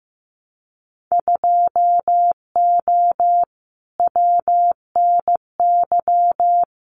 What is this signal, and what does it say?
Morse code, heard as a keyed tone, about 15 wpm, 700 Hz. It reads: 2O WNY
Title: Winter Field Day Exchanges Encore 15wpm